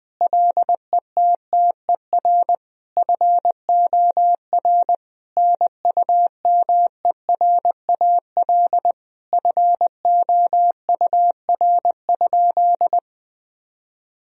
Morse code, heard as a keyed tone, about 21 wpm, 700 Hz. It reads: LETTER FOR NUMERAL FOUR?
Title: Sentences from Top 1000 Words - Review Part 2 - 20wpm